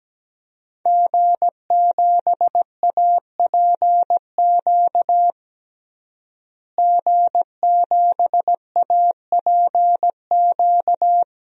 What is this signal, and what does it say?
Morse code, heard as a keyed tone, about 17 wpm, 700 Hz. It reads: G7APQ G7APQ
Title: DXCC-100 Call Signs - 5 Characters - Repeated 3x - 17wpm